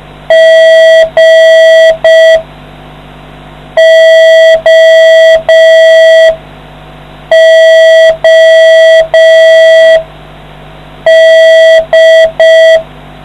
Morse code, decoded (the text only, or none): GOOD